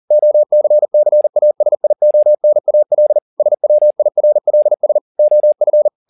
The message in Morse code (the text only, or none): OCCASIONAL SWIRLS OF